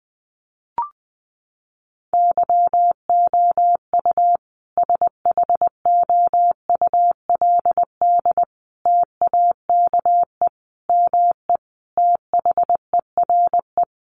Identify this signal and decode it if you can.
Morse code, heard as a keyed tone, about 20 wpm, 700 Hz. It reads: YOU SHOULD TAKE ME THERE
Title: Sentences from Top 300 Words 20wpm